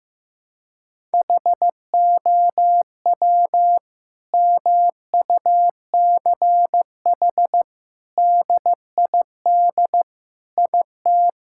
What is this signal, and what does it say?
Morse code, heard as a keyed tone, about 15 wpm, 700 Hz. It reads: HOW MUCH DID IT
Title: Sentences from Top 1000 Words - Review Part 5 - 15wpm